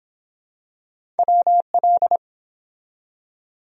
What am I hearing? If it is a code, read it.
Morse code, heard as a keyed tone, about 24 wpm, 700 Hz. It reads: WL